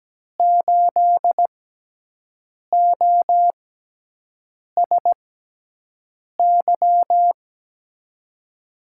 Morse code, heard as a keyed tone, about 17 wpm, spaced as for 6 wpm, 700 Hz. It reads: 8OSY
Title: USA Call Signs - 6 Characters - 6x Character Spacing - 17wpm